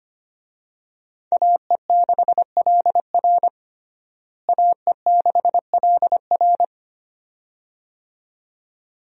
Morse code, heard as a keyed tone, about 25 wpm, 700 Hz. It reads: AE6LR AE6LR
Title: USA Call Signs - 5 Characters - Repeated 2x - 25wpm